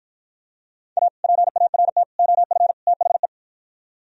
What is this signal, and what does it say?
Morse code, heard as a keyed tone, about 48 wpm, 700 Hz. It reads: A PART OF THE